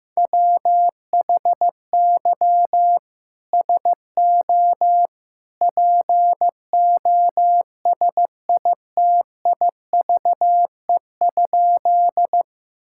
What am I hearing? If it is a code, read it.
Morse code, heard as a keyed tone, about 15 wpm, 700 Hz. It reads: WHY SO POSITIVE?
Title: Sentences from Top 1000 Words - Review Part 5 - 15wpm